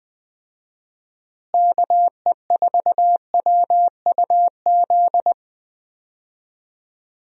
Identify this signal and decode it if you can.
Morse code, heard as a keyed tone, about 20 wpm, 700 Hz. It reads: KE4WUZ
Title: USA Call Signs - 6 Characters - Repeated 3x - 20wpm